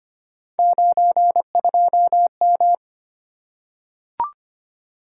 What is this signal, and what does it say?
Morse code, heard as a keyed tone, about 25 wpm, 700 Hz. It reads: 92M
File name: Three Letter-Number Combinations - 25wpm